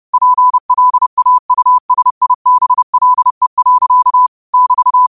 Morse code, heard as a keyed tone, about 30 wpm, 1000 Hz. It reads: PLAUSIBLE. <BT>
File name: qotdfiles / 30wpm